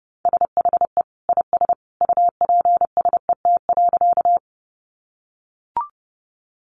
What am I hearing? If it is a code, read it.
Morse code, heard as a keyed tone, about 30 wpm, 700 Hz. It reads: SHE IS UPSET.
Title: Sentences from Top 800 Words 30wpm